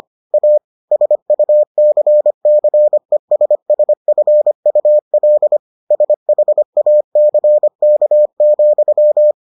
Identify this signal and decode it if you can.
Morse code, heard as a keyed tone, about 25 wpm, 600 Hz. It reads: A SUCCESSFUL SHACK,